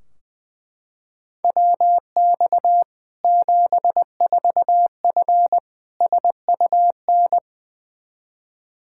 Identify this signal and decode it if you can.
Morse code, heard as a keyed tone, about 21 wpm, 700 Hz. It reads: WX 74F SUN
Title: Practice QSO - Exchange 2 - Info - 20wpm